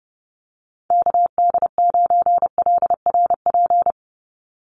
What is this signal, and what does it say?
Morse code, heard as a keyed tone, about 30 wpm, 700 Hz. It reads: KD9LRP
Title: USA Call Signs 30wpm